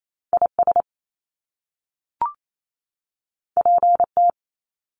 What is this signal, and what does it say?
Morse code, heard as a keyed tone, about 28 wpm, 700 Hz. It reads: IS PT
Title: Top Two Letter Combinations in Words 28wpm